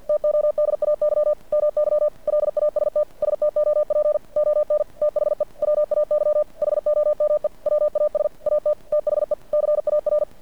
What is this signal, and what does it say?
Morse code, heard as a keyed tone, about 50 wpm, 600 Hz. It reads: TODAY MY LAST STOP ON THE WAY HOME WAS AT THE CAR
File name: archived / CRSnet